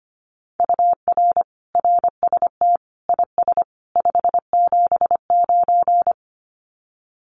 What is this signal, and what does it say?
Morse code, heard as a keyed tone, about 25 wpm, 700 Hz. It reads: UR RST IS 579
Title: Common QSO Phrases 25wpm